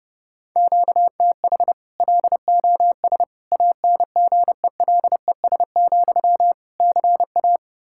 Morse code, heard as a keyed tone, about 32 wpm, 700 Hz. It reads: QTH LOS ANGELES, CA
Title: Practice QSO - Exchange 1 - Beginning - 30wpm